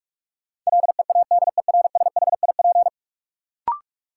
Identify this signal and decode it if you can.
Morse code, heard as a keyed tone, about 42 wpm, 700 Hz. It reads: LEADERSHIP
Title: Sentences from Top 1600 Words - New Word Drill - 45wpm